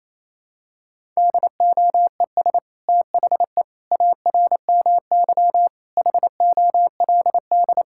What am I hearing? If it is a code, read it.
Morse code, heard as a keyed tone, about 28 wpm, 700 Hz. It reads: DOES THE ARMY HOLD
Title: Sentences from Top 1000 Words - Review Part 6 - 28wpm